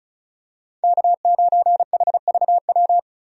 Morse code, heard as a keyed tone, about 35 wpm, 700 Hz. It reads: K9HVW
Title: USA Call Signs - 5 Characters - Repeated 2x - 35wpm